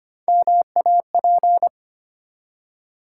Morse code, heard as a keyed tone, about 25 wpm, 700 Hz. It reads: MAP
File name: practice / 25wpm / Easy Words with Hints - 25wpm